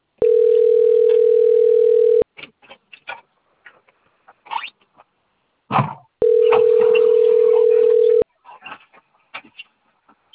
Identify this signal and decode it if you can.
Ringback tone (North America), 440 + 480 Hz.